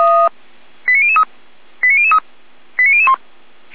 Five-tone selective call (ZVEI), heard as quick runs of five tones, tones of about 70 ms. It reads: 89003 89003 89001